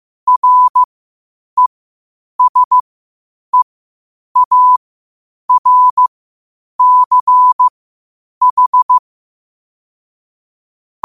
Morse code, heard as a keyed tone, about 15 wpm, spaced as for 8 wpm, 1000 Hz. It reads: RESEARCH
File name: qotdfiles / 8wpm